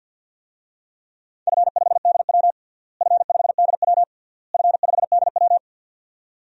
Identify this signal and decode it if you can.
Morse code, heard as a keyed tone, about 46 wpm, 700 Hz. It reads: F5DW F5DW F5DW
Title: DXCC-100 Call Signs - 4 Characters - Repeated 3x - 50wpm